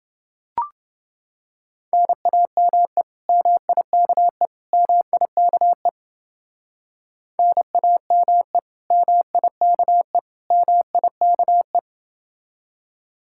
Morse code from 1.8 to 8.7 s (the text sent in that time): NAME MIKE MIKE NAME